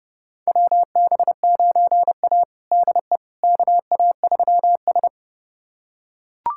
Morse code, heard as a keyed tone, about 29 wpm, 700 Hz. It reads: WB9A DE KA3S
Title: USA Callsign DE Callsign - 4 Characters - 30wpm